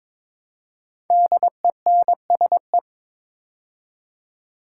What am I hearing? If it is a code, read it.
Morse code, heard as a keyed tone, about 22 wpm, 700 Hz. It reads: DENSE
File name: 5 Letter Words - First 3 Letters Given - 22wpm